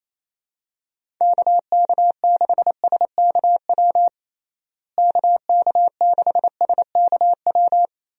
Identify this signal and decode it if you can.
Morse code, heard as a keyed tone, about 28 wpm, 700 Hz. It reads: KK6SKW KK6SKW
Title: USA Call Signs - 6 Characters - Repeated 3x - 28wpm